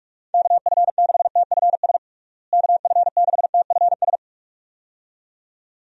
Morse code, heard as a keyed tone, about 42 wpm, 700 Hz. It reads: KF6TFS KF6TFS
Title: USA Call Signs - 6 Characters - Repeated 2x - 45wpm